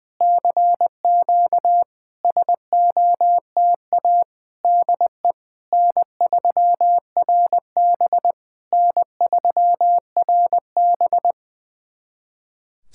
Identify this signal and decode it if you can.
Morse code, heard as a keyed tone, about 20 wpm, 690 Hz. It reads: CQ SOTA DE N3RB N3RB
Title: SOTA QSO Practice 20wpm